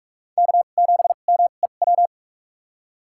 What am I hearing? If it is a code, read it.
Morse code, heard as a keyed tone, about 47 wpm, spaced as for 33 wpm, 700 Hz. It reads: K7MEW
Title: USA Call Signs - 5 Characters - 2x Character Spacing - 45wpm